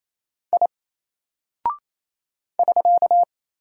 Morse code, heard as a keyed tone, about 28 wpm, 700 Hz. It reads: I <SK>
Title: Single Character (Letter:Number:Punctuation:Prosign) Alt 28wpm